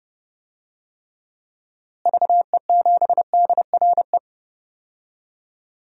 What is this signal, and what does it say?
Morse code, heard as a keyed tone, about 30 wpm, 700 Hz. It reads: VE7DRE
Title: DXCC-100 Call Signs - 6 Characters - Repeated 2x - 30wpm